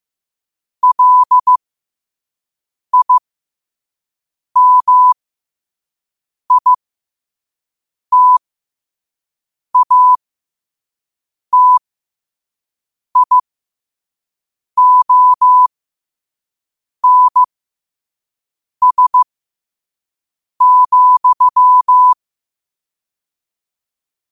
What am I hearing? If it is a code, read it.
Morse code, heard as a keyed tone, about 15 wpm, spaced as for 5 wpm, 1000 Hz. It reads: LIMITATIONS,
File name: qotdfiles / 5wpm